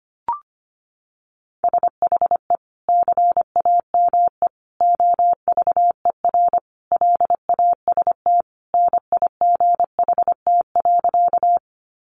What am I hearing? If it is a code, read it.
Morse code, heard as a keyed tone, about 25 wpm, 700 Hz. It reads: SHE CAME OVER LAST NIGHT.
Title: Sentences from Top 300 Words 25wpm